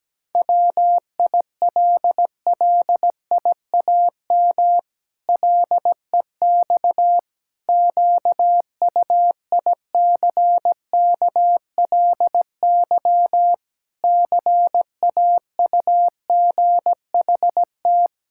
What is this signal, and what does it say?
Morse code, heard as a keyed tone, about 17 wpm, 700 Hz. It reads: WILLIAM LEX QUICKLY CAUGHT
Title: Pangram Challenge v2 - 17wpm